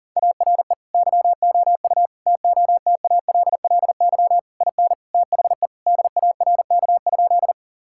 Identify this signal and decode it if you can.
Morse code, heard as a keyed tone, about 40 wpm, 700 Hz. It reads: ARE YOU TOTALLY IN THE DARK?
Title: Sentences from Top 1600 Words 40wpm